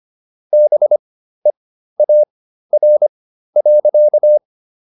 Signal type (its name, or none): Morse code